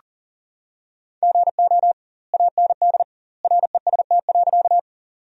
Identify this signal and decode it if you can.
Morse code, heard as a keyed tone, about 40 wpm, 700 Hz. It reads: GO AND REST.
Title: 2x Word-Spacing - Sentences from Top 500 Words - 40wpm